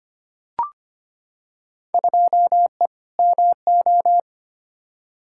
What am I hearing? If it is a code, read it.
Morse code, heard as a keyed tone, about 25 wpm, 700 Hz. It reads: 2E MO
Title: Field Day Exchanges - 25wpm